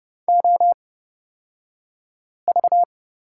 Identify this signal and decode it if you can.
Morse code, heard as a keyed tone, about 31 wpm, spaced as for 5 wpm, 690 Hz. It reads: OV
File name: Lesson 36d - BT - Two Characters 30wpm